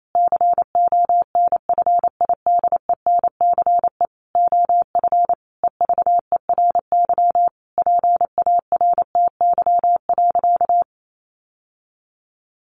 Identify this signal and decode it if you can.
Morse code, heard as a keyed tone, about 28 wpm, 700 Hz. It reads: CONFIDENCE OF EVERY PARTY.
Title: Sentences from Top 1500 Words 28wpm